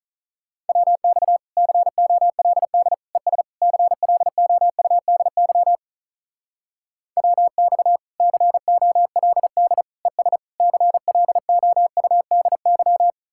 Morse code, heard as a keyed tone, about 45 wpm, 700 Hz. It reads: WX COLD ES CLOUDY WX COLD ES CLOUDY